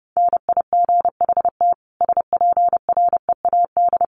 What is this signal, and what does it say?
Morse code, heard as a keyed tone, about 30 wpm, 700 Hz. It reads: NIGHT SPREAD
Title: Sentences from Top 1200 Words 30wpm